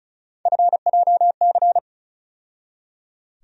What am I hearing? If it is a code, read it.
Morse code, heard as a keyed tone, about 34 wpm, 700 Hz. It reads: FJC
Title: USA Call Sign Postfixes - 3 Letters 35wpm